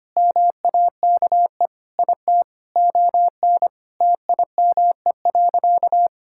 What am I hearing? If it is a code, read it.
Morse code, heard as a keyed tone, about 25 wpm, 700 Hz. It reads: MAKE IT ON TIME.
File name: practice / 25wpm / Sentences with 3-Word Phrases - 25wpm